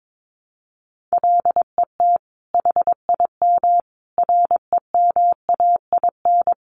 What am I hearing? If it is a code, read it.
Morse code, heard as a keyed tone, about 22 wpm, 700 Hz. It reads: LET HIM REMAIN
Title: Sentences from Top 1200 Words 22wpm